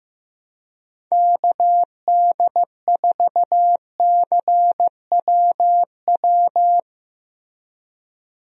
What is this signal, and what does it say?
Morse code, heard as a keyed tone, about 15 wpm, 700 Hz. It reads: KD4CWW
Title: USA Call Signs - 6 Characters - Repeated 3x - 15wpm